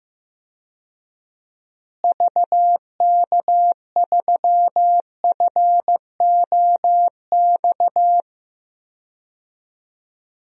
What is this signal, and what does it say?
Morse code, heard as a keyed tone, about 15 wpm, 700 Hz. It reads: VK3FOX
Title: DXCC-100 Call Signs - 6 Characters - Repeated 3x - 15wpm